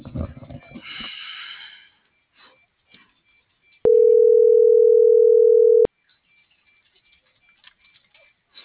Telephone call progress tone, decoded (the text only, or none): ringback tone